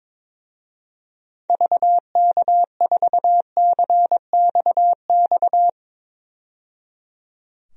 Morse code, heard as a keyed tone, about 21 wpm, 700 Hz. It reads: VK4CXX
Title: DXCC-100 Call Signs - 6 Characters - 22wpm